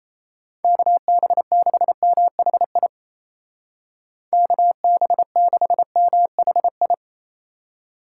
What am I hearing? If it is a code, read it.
Morse code, heard as a keyed tone, about 36 wpm, 700 Hz. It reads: KB6MHI KB6MHI